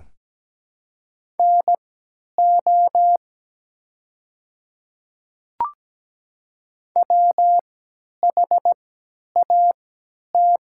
Morse code, heard as a keyed tone, about 17 wpm, spaced as for 10 wpm, 700 Hz. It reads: NO WHAT
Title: 3x Character Spacing - Top 100 Words - 17wpm